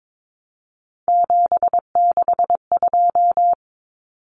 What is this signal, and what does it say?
Morse code, heard as a keyed tone, about 22 wpm, 700 Hz. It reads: 762
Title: Common Cut Numbers - 3-Digit 22wpm